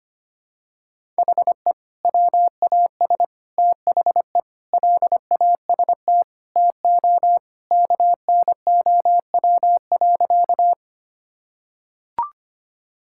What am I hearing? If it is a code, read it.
Morse code, heard as a keyed tone, about 25 wpm, 700 Hz. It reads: HE WAS THE LAST TO KNOW.
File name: practice / 25wpm / Sentences with 2-Word Phrases - 25wpm